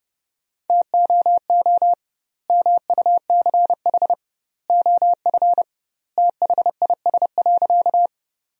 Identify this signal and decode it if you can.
Morse code, heard as a keyed tone, about 32 wpm, 700 Hz. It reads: TOO MUCH OF THIS.
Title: 2x Word-Spacing - Sentences from Top 200 Words - 30wpm